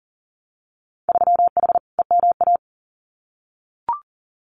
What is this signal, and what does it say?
Morse code, heard as a keyed tone, about 42 wpm, 700 Hz. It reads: 3H EMA
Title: Winter Field Day Exchanges Encore 40wpm